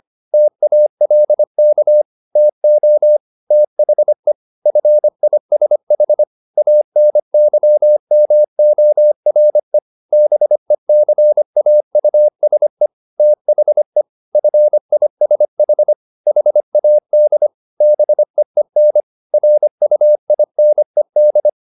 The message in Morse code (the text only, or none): TALK TO THE FISH ANYMORE BECAUSE THE FISH HAD BEEN RUINED